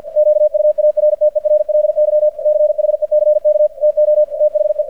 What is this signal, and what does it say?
Morse code, heard as a keyed tone, about 50 wpm, 600 Hz. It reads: QUARTERLY CHECK TO THE
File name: archived / CRSnet